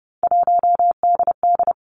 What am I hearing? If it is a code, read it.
Morse code, heard as a keyed tone, about 30 wpm, 700 Hz. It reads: 1DD
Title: Three Challenging Letter-Number Combinations 30wpm